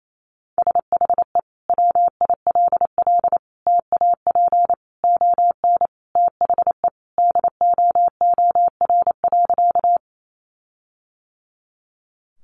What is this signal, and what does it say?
Morse code, heard as a keyed tone, about 28 wpm, 700 Hz. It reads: SHE WILL TAP ON THE DOOR.
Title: Sentences from Top 1400 Words 28wpm